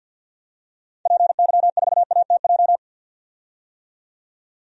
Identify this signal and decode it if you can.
Morse code, heard as a keyed tone, about 49 wpm, 700 Hz. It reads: PY4ATJ
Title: DXCC-100 Call Signs - 6 Characters - 50wpm